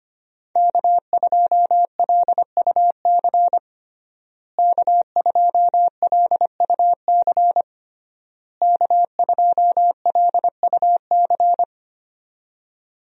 Morse code, heard as a keyed tone, about 25 wpm, 700 Hz. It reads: K2LUC K2LUC K2LUC
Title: USA Call Signs - 5 Characters - Repeated 3x - 25wpm